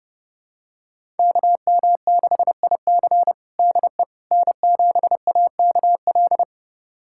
Morse code, heard as a keyed tone, about 30 wpm, 700 Hz. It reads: KM6IC DE N7AKL
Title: USA Callsign DE Callsign - 5 Characters - 30wpm